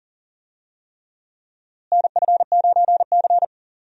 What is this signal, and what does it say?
Morse code, heard as a keyed tone, about 40 wpm, 700 Hz. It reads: NF9C
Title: USA Call Signs - 4 Characters - Repeated 3x - 40wpm